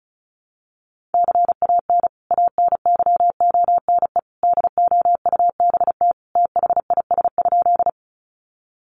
Morse code, heard as a keyed tone, about 34 wpm, 700 Hz. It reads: CAN ANYONE DOUBT THIS?
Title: Sentences from Top 1500 Words 35wpm